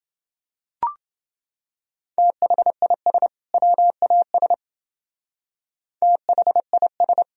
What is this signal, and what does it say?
Morse code, heard as a keyed tone, about 28 wpm, 700 Hz. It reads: THIS WAS THIS